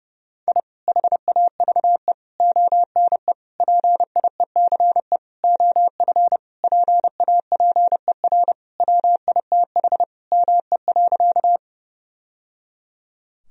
Morse code, heard as a keyed tone, about 29 wpm, 700 Hz. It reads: I HAVE ONE PIECE OF PAPER WITH ME.
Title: Sentences from Top 1000 Words - Review Part 2 - 30wpm